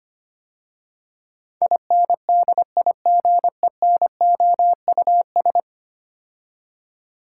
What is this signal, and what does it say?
Morse code, heard as a keyed tone, about 25 wpm, 700 Hz. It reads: INDIGENOUS
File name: 10 Letter Words - First 3 Letters Given - 25wpm